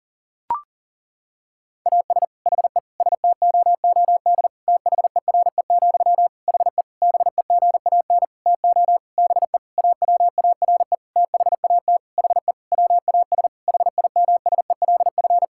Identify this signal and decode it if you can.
Morse code, heard as a keyed tone, about 37 wpm, 700 Hz. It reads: AS HE STOOD THERE, HE BEGAN TO BE AWARE THAT HE WAS HIMSELF
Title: Sentences from Top 1600 Words 40wpm